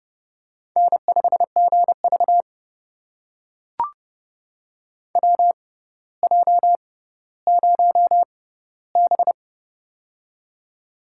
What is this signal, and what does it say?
Morse code, heard as a keyed tone, about 30 wpm, spaced as for 15 wpm, 700 Hz. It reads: N5GV WJ0B
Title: USA Call Signs - 4 Characters - 6x Character Spacing - 30wpm